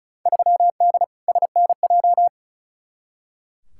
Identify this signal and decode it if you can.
Morse code, heard as a keyed tone, about 36 wpm, 700 Hz. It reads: 3D SNJ
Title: Field Day Exchanges Encore - 35wpm